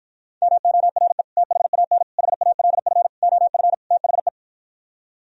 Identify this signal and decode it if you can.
Morse code, heard as a keyed tone, about 52 wpm, 700 Hz. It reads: MORE THAN HALF OF THE